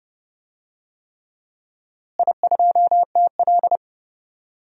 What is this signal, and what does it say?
Morse code, heard as a keyed tone, about 30 wpm, 700 Hz. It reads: I2TL
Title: DXCC-100 Call Signs - 4 Characters - Repeated 3x - 30wpm